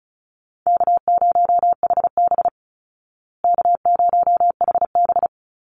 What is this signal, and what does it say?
Morse code, heard as a keyed tone, about 34 wpm, 700 Hz. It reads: K0HB K0HB
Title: Call Signs Repeated 2x - 35wpm